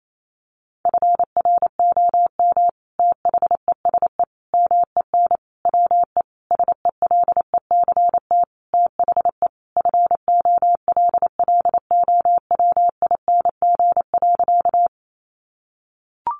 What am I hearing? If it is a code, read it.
Morse code, heard as a keyed tone, about 28 wpm, 700 Hz. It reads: FROM THESE MEN WE SELECT THE FOLLOWING.
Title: Sentences from Top 1300 Words 28wpm